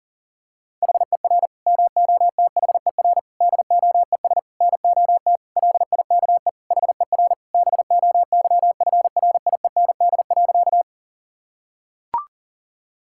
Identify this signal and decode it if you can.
Morse code, heard as a keyed tone, about 40 wpm, 700 Hz. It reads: HER MOTHER DOES NOT LIKE HER BOYFRIEND.
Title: Sentences from Top 1000 Words - Review Part 5 - 40wpm